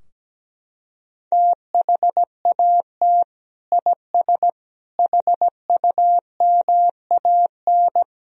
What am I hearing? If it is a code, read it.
Morse code, heard as a keyed tone, about 17 wpm, 700 Hz. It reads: THAT IS HUMAN
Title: Sentences from Top 1000 Words - Review Part 3 - 17wpm